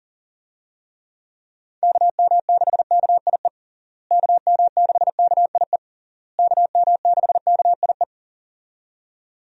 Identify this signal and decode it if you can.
Morse code, heard as a keyed tone, about 40 wpm, 700 Hz. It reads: KM6KIE KM6KIE KM6KIE
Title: USA Call Signs - 6 Characters - Repeated 3x - 40wpm